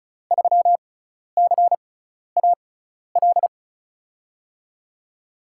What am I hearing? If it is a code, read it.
Morse code, heard as a keyed tone, about 36 wpm, spaced as for 12 wpm, 700 Hz. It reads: 3CAL
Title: USA Call Signs - 5 Characters - 6x Character Spacing - 35wpm